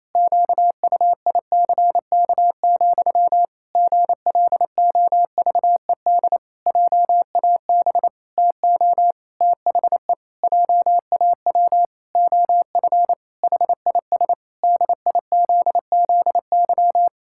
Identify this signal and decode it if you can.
Morse code, heard as a keyed tone, about 28 wpm, 700 Hz. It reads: QUICK, GLOVED JAB TO THE JAW OF HIS DIZZY
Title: Pangram Challenge v2 - 28wpm